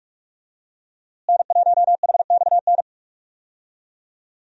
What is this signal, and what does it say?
Morse code, heard as a keyed tone, about 46 wpm, 700 Hz. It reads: N1HXN
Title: USA Call Signs - 5 Characters - Repeated 3x - 45wpm